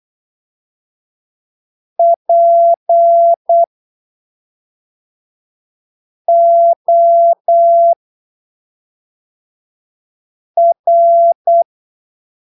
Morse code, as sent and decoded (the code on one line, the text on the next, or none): .--. --- .-.
POR